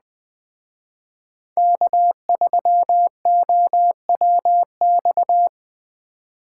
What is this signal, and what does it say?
Morse code, heard as a keyed tone, about 21 wpm, 700 Hz. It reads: K3OWX
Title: Speed Racing - USA 1x3 Call Signs 17wpm